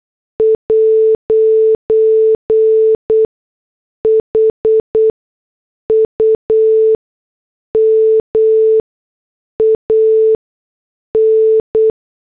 Morse code, heard as a keyed tone, about 8 wpm, spaced as for 6 wpm, 440 Hz. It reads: 'HUMAN
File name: morse / headlines